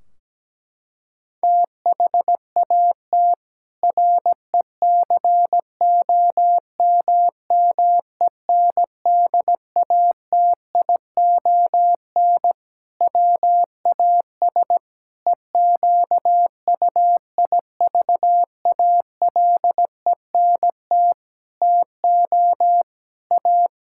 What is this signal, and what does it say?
Morse code, heard as a keyed tone, about 17 wpm, 700 Hz. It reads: THAT RECOMMENDATION WAS EQUIVALENT TO A
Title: Sentences from Top 1600 Words 17wpm